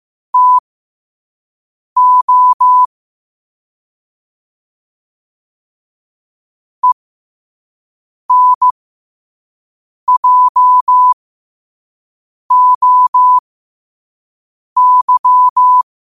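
Morse code, heard as a keyed tone, about 15 wpm, spaced as for 5 wpm, 1000 Hz. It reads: TO ENJOY
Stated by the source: Morse code quote practice at 5 words per minute.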